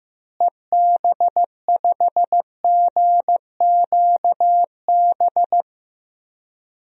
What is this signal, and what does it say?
Morse code, heard as a keyed tone, about 15 wpm, 700 Hz. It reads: EB5GQB
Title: DXCC-100 Call Signs - 6 Characters - Repeated 3x - 15wpm